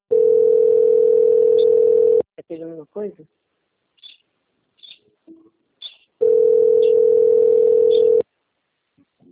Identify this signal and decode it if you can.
Ringback tone (North America), 440 + 480 Hz, with speech.
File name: out- - - - - .36984